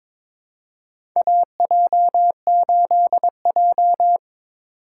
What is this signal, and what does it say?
Morse code, heard as a keyed tone, about 21 wpm, 700 Hz. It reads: AJ8J
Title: USA Call Signs - 4 Characters - 3x Character Spacing - 22wpm